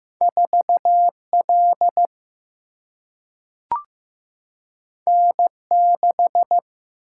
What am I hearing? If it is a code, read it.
Morse code, heard as a keyed tone, about 15 wpm, 700 Hz. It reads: 4 L N 6
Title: Single-Letter Single-Number 15wpm